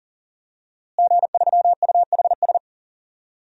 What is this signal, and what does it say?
Morse code, heard as a keyed tone, about 37 wpm, 700 Hz. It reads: G3UHS
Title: DXCC-100 Call Signs - 5 Characters - Repeated 2x - 40wpm